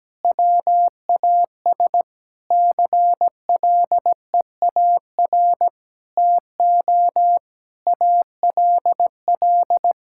Morse code, heard as a keyed tone, about 17 wpm, 700 Hz. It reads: WAS CLEAR TO ALL
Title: Sentences from Top 1000 Words - Review Part 6 - 17wpm